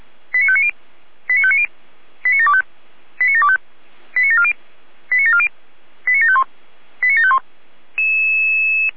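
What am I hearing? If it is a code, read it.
Five-tone selective call (ZVEI), heard as quick runs of five tones, tones of about 70 ms. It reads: 89590 89590 89725 89725 89740 89740 89751 89751
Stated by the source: A ZVEI five-tone selective call, code 89735, received with an SDR.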